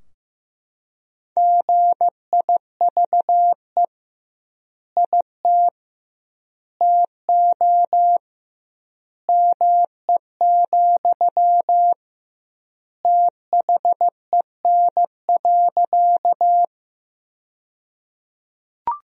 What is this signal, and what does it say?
Morse code, heard as a keyed tone, about 15 wpm, 700 Hz. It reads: GIVE IT TO ME, THEN.
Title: 2x Word-Spacing - Sentences from Top 200 Words - 15wpm